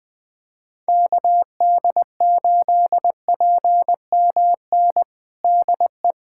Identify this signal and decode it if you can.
Morse code, heard as a keyed tone, about 21 wpm, 700 Hz. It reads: KD8PMN DE
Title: USA Callsign DE Callsign - 6 Characters - 20wpm